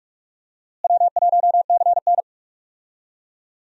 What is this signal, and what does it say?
Morse code, heard as a keyed tone, about 44 wpm, 700 Hz. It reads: W1CN
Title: USA Call Signs - 4 Characters - Repeated 3x - 45wpm